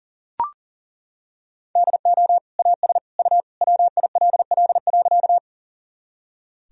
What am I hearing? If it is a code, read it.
Morse code, heard as a keyed tone, about 37 wpm, 700 Hz. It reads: DO AS U WILL.
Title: Sentences from Top 100 Words and Abbreviations 40wpm